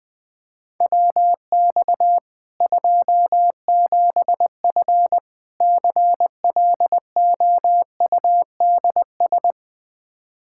Morse code, heard as a keyed tone, about 20 wpm, 700 Hz. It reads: WX 27F CLOUDS
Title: K1USN SST QSO Practice 20wpm